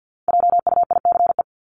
Morse code, heard as a keyed tone, about 50 wpm, 700 Hz. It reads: JUICE
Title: Top 1251-1500 Words 50wpm